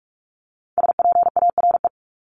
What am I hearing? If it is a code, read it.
Morse code, heard as a keyed tone, about 44 wpm, 700 Hz. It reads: SPARE
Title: Top 1751-2000 Words 45wpm